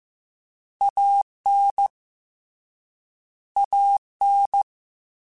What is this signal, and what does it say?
Morse code, heard as a keyed tone, about 15 wpm, 800 Hz. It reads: AN AN
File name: Short Runs / Short Runs-mp3s / ShortContrasts @13